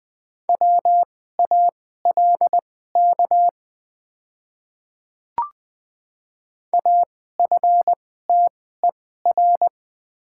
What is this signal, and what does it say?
Morse code, heard as a keyed tone, about 20 wpm, spaced as for 14 wpm, 700 Hz. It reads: WALK AFTER
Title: 2x Character Spacing - Top 500 Words - 20wpm